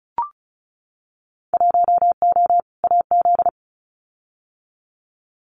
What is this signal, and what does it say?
Morse code, heard as a keyed tone, about 34 wpm, 700 Hz. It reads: 1O AZ
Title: Winter Field Day Exchanges 35wpm